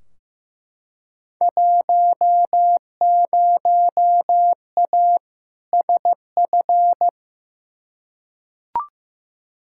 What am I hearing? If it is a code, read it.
Morse code, heard as a keyed tone, about 15 wpm, 700 Hz. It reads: 10A SF
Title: Field Day Exchanges Encore - 15wpm